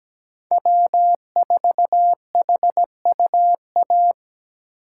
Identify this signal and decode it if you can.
Morse code, heard as a keyed tone, about 17 wpm, 700 Hz. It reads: W4HUA
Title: USA Call Signs - 5 Characters - 8x Character Spacing - 17wpm